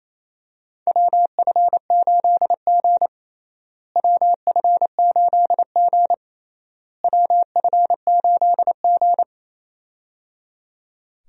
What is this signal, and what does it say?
Morse code, heard as a keyed tone, about 28 wpm, 700 Hz. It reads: WF8G WF8G WF8G
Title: USA Call Signs - 4 Characters - Repeated 3x - 28wpm